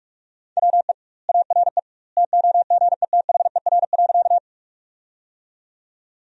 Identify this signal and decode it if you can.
Morse code, heard as a keyed tone, about 45 wpm, 700 Hz. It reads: WE ARE TOGETHER.
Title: 2x Word-Spacing - Sentences from Top 300 Words - 45wpm